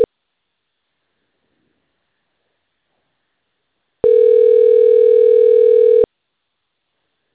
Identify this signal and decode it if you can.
Ringback tone (North America), 440 + 480 Hz.